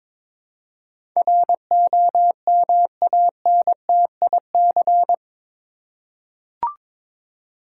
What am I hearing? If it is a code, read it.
Morse code, heard as a keyed tone, about 21 wpm, 700 Hz. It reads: ROMANTIC
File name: practice / 22wpm / Eight Letter Words - 22wpm